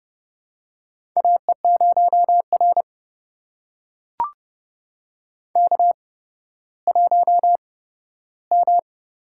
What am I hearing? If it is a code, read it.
Morse code, heard as a keyed tone, about 29 wpm, spaced as for 14 wpm, 700 Hz. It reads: AE0R K1M
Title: USA Call Signs - 4 Characters - 8x Character Spacing - 30wpm